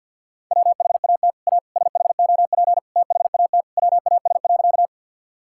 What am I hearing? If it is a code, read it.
Morse code, heard as a keyed tone, about 50 wpm, 700 Hz. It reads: WHAT A SHOP THAT WAS.
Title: Sentences from Top 1600 Words 50wpm